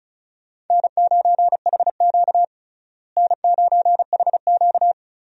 Morse code, heard as a keyed tone, about 36 wpm, 700 Hz. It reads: N9HQ N9HQ
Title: USA Call Signs - 4 Characters - Repeated 2x - 35wpm